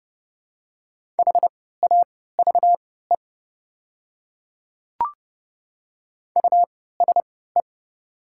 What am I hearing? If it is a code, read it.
Morse code, heard as a keyed tone, about 30 wpm, spaced as for 17 wpm, 690 Hz. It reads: HAVE USE
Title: 3x Character Spacing - Top 100 Words - 30wpm